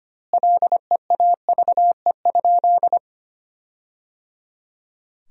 Morse code, heard as a keyed tone, about 25 wpm, 700 Hz. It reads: LEAVE?
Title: 2x Word-Spacing - Sentences from Top 400 Words - 25wpm